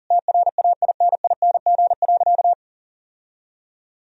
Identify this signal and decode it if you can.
Morse code, heard as a keyed tone, about 40 wpm, 700 Hz. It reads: TRAINING.
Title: Sentences from Top 1000 Words - Review Part 4 - 40wpm